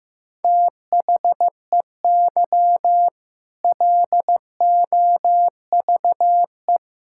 Morse code, heard as a keyed tone, about 15 wpm, 700 Hz. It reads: THEY LOVE
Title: Sentences from Top 1600 Words 15wpm